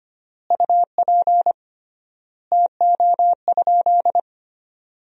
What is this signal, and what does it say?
Morse code, heard as a keyed tone, about 25 wpm, 700 Hz. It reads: UP TO?
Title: 3x Word-Spacing - Sentences from Top 100 Words - 25wpm